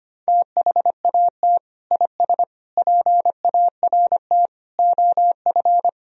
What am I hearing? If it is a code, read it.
Morse code, heard as a keyed tone, about 25 wpm, 700 Hz. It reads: THAT IS PART OF
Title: Sentences from Top 1000 Words - Review Part 1 - 25wpm